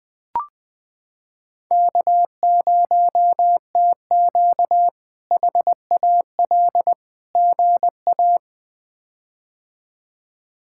Morse code, heard as a keyed tone, about 20 wpm, 700 Hz. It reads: K0TQ HAL GA
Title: K1USN SST QSO Practice - 20wpm